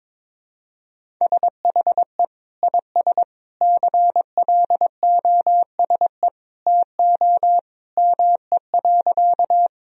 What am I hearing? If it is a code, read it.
Morse code, heard as a keyed tone, about 22 wpm, 700 Hz. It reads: SHE IS CLOSE TO ME.
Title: Sentences from Top 1000 Words - Review Part 1 - 22wpm